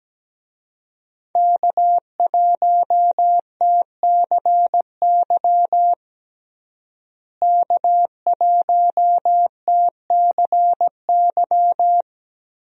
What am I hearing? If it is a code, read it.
Morse code, heard as a keyed tone, about 17 wpm, 700 Hz. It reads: K1TCY K1TCY
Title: USA Call Signs - 5 Characters - Repeated 3x - 17wpm